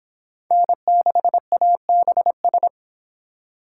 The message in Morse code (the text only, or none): N6ABS